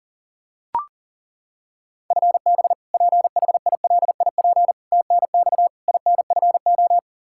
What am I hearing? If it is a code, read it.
Morse code, heard as a keyed tone, about 40 wpm, 700 Hz. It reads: FB PHILIP TNX INFO
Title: Practice QSO - Exchange 3 - More Info - 40wpm